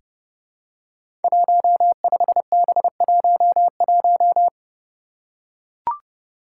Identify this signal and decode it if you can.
Morse code, heard as a keyed tone, about 30 wpm, 700 Hz. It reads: 15B11
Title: Five Challenging Letter-Number Combinations 30wpm